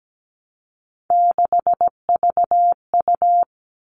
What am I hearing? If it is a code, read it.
Morse code, heard as a keyed tone, about 17 wpm, 700 Hz. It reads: 6VU
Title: All Cut Numbers - 3-Digit 17wpm